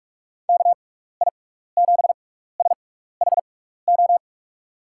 Morse code, heard as a keyed tone, about 46 wpm, spaced as for 15 wpm, 700 Hz. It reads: KI7SHO
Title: USA Call Signs - 6 Characters - 6x Character Spacing - 45wpm